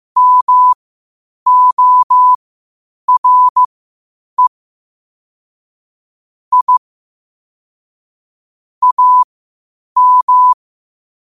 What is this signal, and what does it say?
Morse code, heard as a keyed tone, about 15 wpm, spaced as for 8 wpm, 1000 Hz. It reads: MORE I AM
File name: qotdfiles / 8wpm